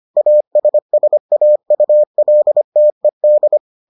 Morse code, heard as a keyed tone, about 25 wpm, 600 Hz. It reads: ASSAULTED